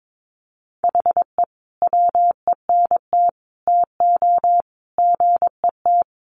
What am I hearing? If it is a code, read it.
Morse code, heard as a keyed tone, about 22 wpm, 700 Hz. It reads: HE WENT TO GET
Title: Sentences from Top 1500 Words 22wpm